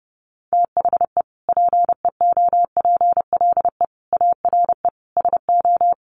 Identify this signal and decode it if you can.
Morse code, heard as a keyed tone, about 30 wpm, 700 Hz. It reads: THE PEOPLE ARE SO
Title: Sentences from Top 1000 Words 30wpm